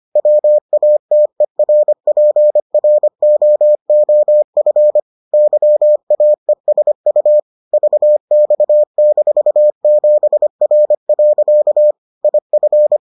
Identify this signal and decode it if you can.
Morse code, heard as a keyed tone, about 25 wpm, 600 Hz. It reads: WATERPROOF YAESU VX-7R. IF